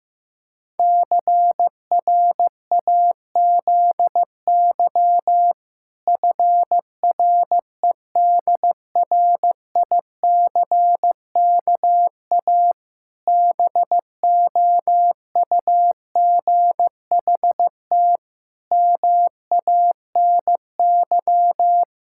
Morse code, heard as a keyed tone, about 15 wpm, 700 Hz. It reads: CRAZY FREDRICKA BOUGHT MANY
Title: Pangram Challenge v2 - 15wpm